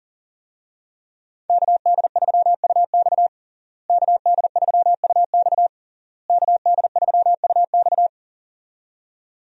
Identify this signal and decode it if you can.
Morse code, heard as a keyed tone, about 40 wpm, 700 Hz. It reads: KD3UX KD3UX KD3UX
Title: USA Call Signs - 5 Characters - Repeated 3x - 40wpm